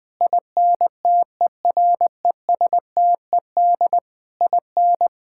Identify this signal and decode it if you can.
Morse code, heard as a keyed tone, about 20 wpm, 700 Hz. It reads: INTERESTED IN
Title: Sentences from Top 1600 Words 20wpm